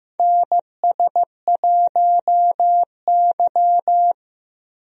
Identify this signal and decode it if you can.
Morse code, heard as a keyed tone, about 15 wpm, 700 Hz. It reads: NS1Y
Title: USA Call Signs - 4 Characters - Repeated 3x - 15wpm